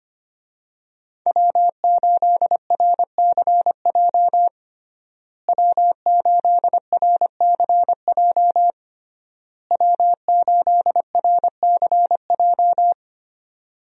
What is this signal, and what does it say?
Morse code, heard as a keyed tone, about 25 wpm, 700 Hz. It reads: W8RCJ W8RCJ W8RCJ
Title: USA Call Signs - 5 Characters - Repeated 3x - 25wpm